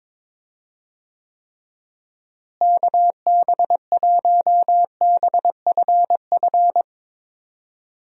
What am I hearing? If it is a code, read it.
Morse code, heard as a keyed tone, about 22 wpm, 700 Hz. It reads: KB1BFF
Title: USA Call Signs - 6 Characters - Repeated 3x - 22wpm